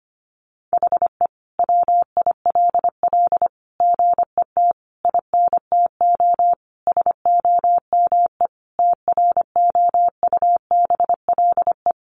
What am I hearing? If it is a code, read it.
Morse code, heard as a keyed tone, about 25 wpm, 700 Hz. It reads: HE WILL GET INTO SOME TROUBLE
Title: Sentences from Top 1500 Words 25wpm